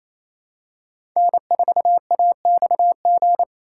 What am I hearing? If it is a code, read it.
Morse code, heard as a keyed tone, about 28 wpm, 700 Hz. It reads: N4AXG
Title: USA Call Signs - 5 Characters - 2x Character Spacing - 28wpm